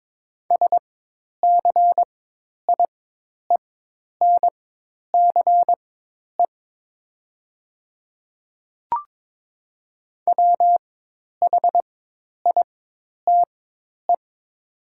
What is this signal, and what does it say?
Morse code, heard as a keyed tone, about 22 wpm, spaced as for 10 wpm, 700 Hz. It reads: SCIENCE WHITE
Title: 4x Character Spacing - Top 500 Words - 22wpm